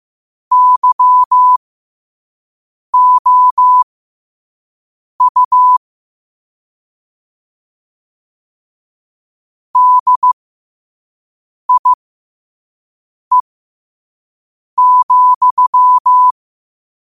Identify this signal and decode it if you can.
Morse code, heard as a keyed tone, about 15 wpm, spaced as for 5 wpm, 1000 Hz. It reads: YOU DIE,
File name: qotdfiles / 5wpm